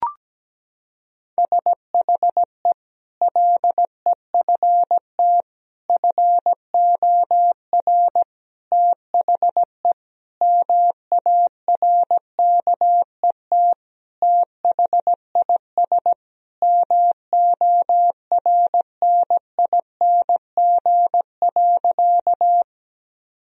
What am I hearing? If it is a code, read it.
Morse code, heard as a keyed tone, about 17 wpm, 700 Hz. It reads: SHE LEFT FOR THE MARKET THIS MORNING.
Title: Sentences from Top 1000 Words - Review Part 3 - 17wpm